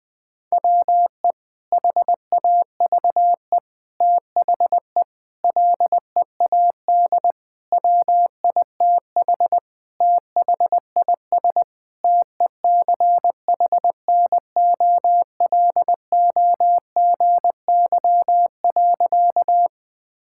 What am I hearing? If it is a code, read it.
Morse code, heard as a keyed tone, about 20 wpm, 700 Hz. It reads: WE HAVE THE LEAD WITH THIS TECHNOLOGY.
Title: Sentences from Top 1000 Words - Review Part 5 - 20wpm